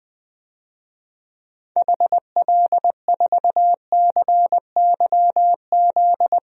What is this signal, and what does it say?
Morse code, heard as a keyed tone, about 20 wpm, 700 Hz. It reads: HL4CYZ
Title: DXCC-100 Call Signs - 6 Characters - 20wpm